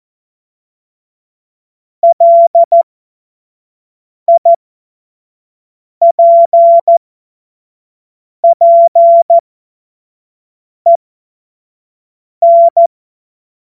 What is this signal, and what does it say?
Morse code, heard as a keyed tone, about 14 wpm, spaced as for 5 wpm, 680 Hz. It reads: LIPPEN